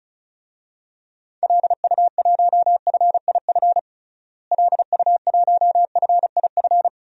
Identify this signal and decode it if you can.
Morse code, heard as a keyed tone, about 36 wpm, 700 Hz. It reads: LU1FIF LU1FIF
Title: DXCC-100 Call Signs - 6 Characters - Repeated 2x - 35wpm